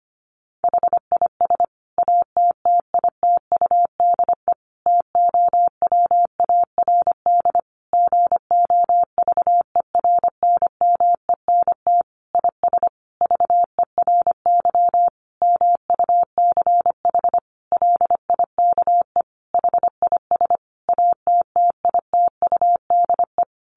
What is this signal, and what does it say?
Morse code, heard as a keyed tone, about 25 wpm, 700 Hz. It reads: HIS ATTITUDE TOWARD GOVERNMENT IS VERY MUCH LIKE HIS ATTITUDE
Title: Sentences from Top 1300 Words 25wpm